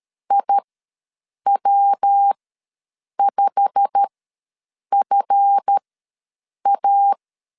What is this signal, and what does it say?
Morse code, heard as a keyed tone, about 13 wpm, spaced as for 7 wpm, 800 Hz. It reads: IW5FA
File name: LEZIONI / 8 YJ45 / 8 YJ45 ws 12wpm 3146